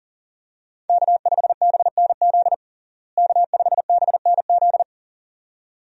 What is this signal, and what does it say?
Morse code, heard as a keyed tone, about 40 wpm, 700 Hz. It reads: K5BNZ K5BNZ
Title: USA Call Signs - 5 Characters - Repeated 2x - 40wpm